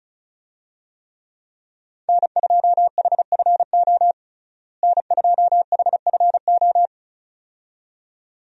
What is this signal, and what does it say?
Morse code, heard as a keyed tone, about 34 wpm, 700 Hz. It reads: N2HFO N2HFO
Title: USA Call Signs - 5 Characters - Repeated 2x - 35wpm